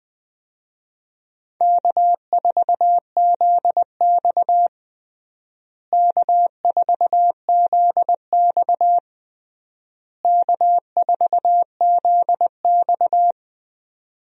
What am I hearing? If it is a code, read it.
Morse code, heard as a keyed tone, about 20 wpm, 700 Hz. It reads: K4ZX K4ZX K4ZX
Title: USA Call Signs - 4 Characters - Repeated 3x - 20wpm